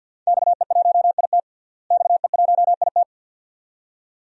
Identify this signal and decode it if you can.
Morse code, heard as a keyed tone, about 48 wpm, 700 Hz. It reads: XE1IT XE1IT
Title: DXCC-100 Call Signs - 5 Characters - Repeated 2x - 50wpm